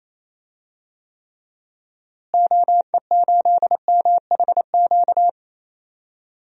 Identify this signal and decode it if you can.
Morse code, heard as a keyed tone, about 28 wpm, 700 Hz. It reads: OE8MHQ
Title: DXCC-100 Call Signs - 6 Characters - Repeated 2x - 28wpm